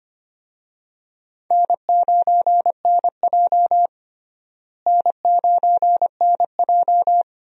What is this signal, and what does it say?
Morse code, heard as a keyed tone, about 25 wpm, 700 Hz. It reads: N9NJ N9NJ
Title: USA Call Signs - 4 Characters - Repeated 2x - 25wpm